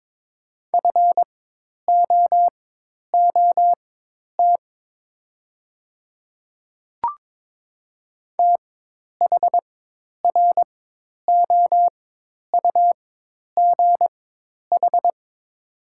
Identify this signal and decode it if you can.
Morse code, heard as a keyed tone, about 22 wpm, spaced as for 10 wpm, 700 Hz. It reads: FOOT THROUGH
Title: 4x Character Spacing - Top 500 Words - 22wpm